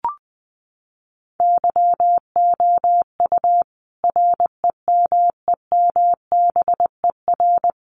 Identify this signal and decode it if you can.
Morse code, heard as a keyed tone, about 20 wpm, 700 Hz. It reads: YOU REMEMBER
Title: Sentences from Top 400 Words 20wpm